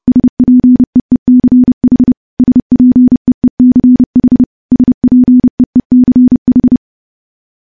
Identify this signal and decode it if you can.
Morse code, heard as a keyed tone, about 30 wpm, 250 Hz. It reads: SPEECH SPEECH SPEECH
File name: morse / morse-training-30WPM